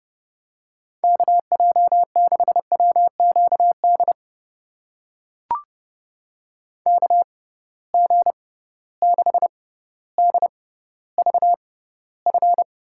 Morse code, heard as a keyed tone, about 32 wpm, spaced as for 16 wpm, 700 Hz. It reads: KJ6WQD KG6DVF
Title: USA Call Signs - 6 Characters - 6x Character Spacing - 30wpm